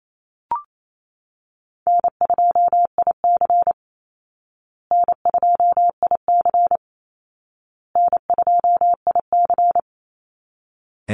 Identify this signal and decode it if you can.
Morse code, heard as a keyed tone, about 28 wpm, 700 Hz. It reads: N2IC N2IC N2IC
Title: Call Signs Repeated 2x - 28wpm